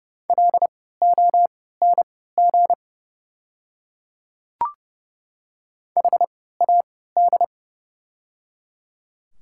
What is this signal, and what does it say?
Morse code, heard as a keyed tone, about 30 wpm, spaced as for 17 wpm, 700 Hz. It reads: LONG HAD
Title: 3x Character Spacing - Top 100 Words - 30wpm